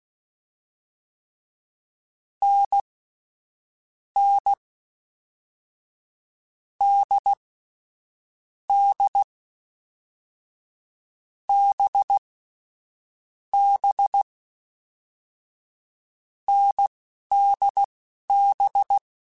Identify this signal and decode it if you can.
Morse code, heard as a keyed tone, about 16 wpm, spaced as for 11 wpm, 800 Hz. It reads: N N D D B B NDB
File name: Short Runs / Short Runs-mp3s / awj1ndb6gz7 @10